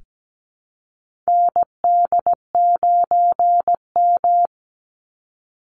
Morse code, heard as a keyed tone, about 17 wpm, 700 Hz. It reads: ND9M
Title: Call Signs Repeated 2x - 17wpm